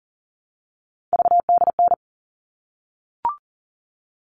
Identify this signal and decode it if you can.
Morse code, heard as a keyed tone, about 40 wpm, 700 Hz. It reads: VDN
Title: All Cut Numbers - 3-Digit 40wpm